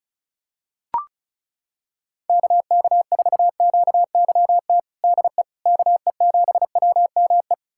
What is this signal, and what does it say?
Morse code, heard as a keyed tone, about 34 wpm, 700 Hz. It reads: KK4QYT DE KE7WME
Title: USA Callsign DE Callsign - 6 Characters - 35wpm